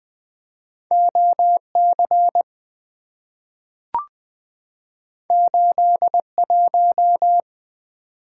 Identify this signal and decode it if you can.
Morse code, heard as a keyed tone, about 20 wpm, 700 Hz. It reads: OC 81
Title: Two Letter-Number Combinations - 20wpm